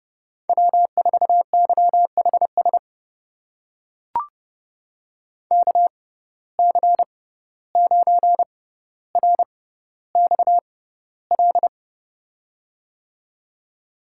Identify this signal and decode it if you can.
Morse code, heard as a keyed tone, about 30 wpm, spaced as for 15 wpm, 700 Hz. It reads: W4YHS KC9RXL
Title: USA Call Signs - 6x Character Spacing - 30wpm